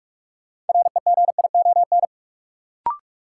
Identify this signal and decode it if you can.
Morse code, heard as a keyed tone, about 45 wpm, 700 Hz. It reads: REGION
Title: Sentences from Top 1500 Words - New Word Drill Alt - 45wpm